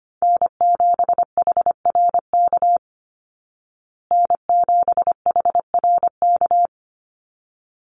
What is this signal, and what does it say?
Morse code, heard as a keyed tone, about 25 wpm, 700 Hz. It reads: N7HRK N7HRK
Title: Call Signs Repeated 3x - 25wpm